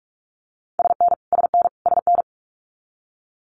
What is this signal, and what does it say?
Morse code, heard as a keyed tone, about 46 wpm, 700 Hz. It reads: SN SN SN
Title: Multi-letter Phonemes Repeated 3x 45wpm